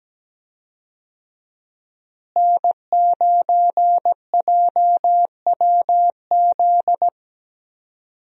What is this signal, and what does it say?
Morse code, heard as a keyed tone, about 17 wpm, 700 Hz. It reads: N9JWZ
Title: USA Call Signs - 5 Characters - Repeated 2x - 17wpm